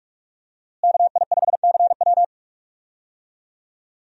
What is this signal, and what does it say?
Morse code, heard as a keyed tone, about 44 wpm, 700 Hz. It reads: KI5CW
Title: USA Call Signs - 5 Characters - Repeated 2x - 45wpm